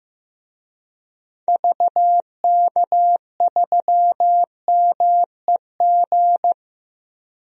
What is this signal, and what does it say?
Morse code, heard as a keyed tone, about 15 wpm, 700 Hz. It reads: VK3MEG
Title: DXCC-100 Call Signs - 6 Characters - Repeated 3x - 15wpm